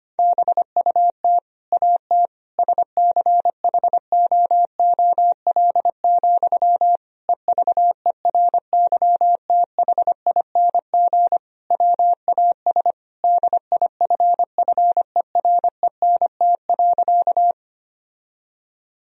Morse code, heard as a keyed tone, about 25 wpm, 700 Hz. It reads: BUT AT SCHOOL, EVERYTHING WAS DIFFERENT.
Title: Sentences from Top 1000 Words - Review Part 3 - 25wpm